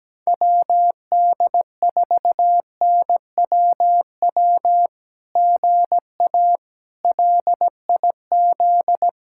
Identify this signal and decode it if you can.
Morse code, heard as a keyed tone, about 17 wpm, 700 Hz. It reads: WD4NWW GA LIZ
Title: SOTA QSO Practice 17wpm